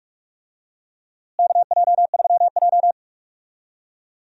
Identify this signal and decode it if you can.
Morse code, heard as a keyed tone, about 46 wpm, 700 Hz. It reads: KJ3J
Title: USA Call Signs - 4 Characters - Repeated 3x - 45wpm